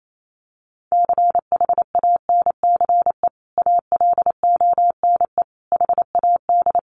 Morse code, heard as a keyed tone, about 28 wpm, 700 Hz. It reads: CHANCE ALONE HAD
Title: Sentences from Top 1000 Words 28wpm